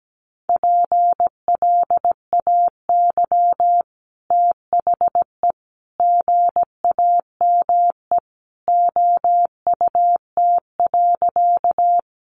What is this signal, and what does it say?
Morse code, heard as a keyed tone, about 17 wpm, 700 Hz. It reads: PLAY THE GAME OUT.
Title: Sentences from Top 500 Words 17wpm